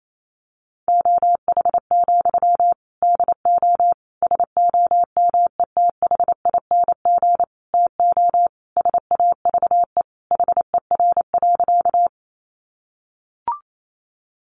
Morse code, heard as a keyed tone, about 28 wpm, 700 Hz. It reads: OH, DO SOMETHING TO SAVE HER.
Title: Sentences from Top 1100 Words 28wpm